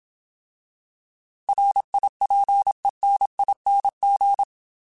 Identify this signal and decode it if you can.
Morse code, heard as a keyed tone, about 27 wpm, 800 Hz. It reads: RIPENING
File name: Alphabetical / Alphabetical mp3s / Rs @25